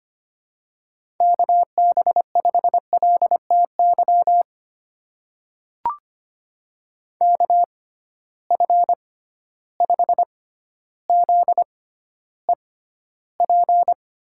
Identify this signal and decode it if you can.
Morse code, heard as a keyed tone, about 25 wpm, spaced as for 13 wpm, 700 Hz. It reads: KB5LTY KF5ZEP
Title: USA Call Signs - 6 Characters - 6x Character Spacing - 25wpm